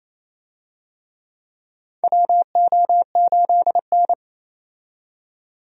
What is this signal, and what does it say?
Morse code, heard as a keyed tone, about 28 wpm, 700 Hz. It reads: WO8N
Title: USA Call Signs - 4 Characters - 28wpm